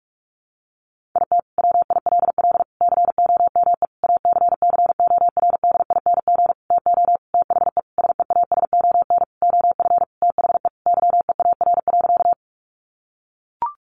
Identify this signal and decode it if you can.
Morse code, heard as a keyed tone, about 44 wpm, 700 Hz. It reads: IT WILL COME ACCORDING TO THE SEASON OF THE YEAR.
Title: Sentences from Top 1400 Words 45wpm